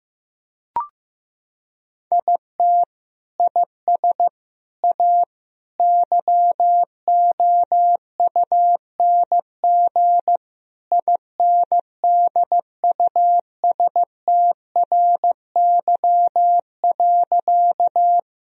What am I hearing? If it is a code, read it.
Morse code, heard as a keyed tone, about 16 wpm, 700 Hz. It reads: IT IS A YOUNG INDUSTRY.
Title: Sentences from Top 1000 Words - Review Part 4 - 15wpm